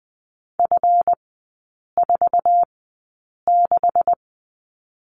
Morse code, heard as a keyed tone, about 21 wpm, spaced as for 8 wpm, 700 Hz. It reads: F46
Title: 138 Lesson 23e - Letter f - Three Characters 20wpm